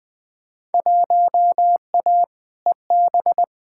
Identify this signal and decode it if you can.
Morse code, heard as a keyed tone, about 20 wpm, 700 Hz. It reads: 1A EB
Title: Field Day Exchanges - 20wpm